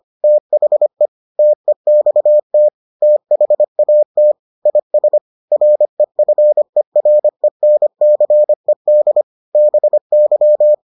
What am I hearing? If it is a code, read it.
Morse code, heard as a keyed tone, about 25 wpm, 600 Hz. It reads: THE TEXT THAT IS REFERENCED BY